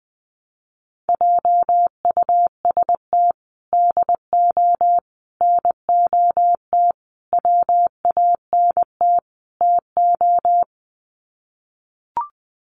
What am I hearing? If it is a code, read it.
Morse code, heard as a keyed tone, about 20 wpm, 700 Hz. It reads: JUST DO NOT WANT TO
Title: Sets of 5 Words Encore 20wpm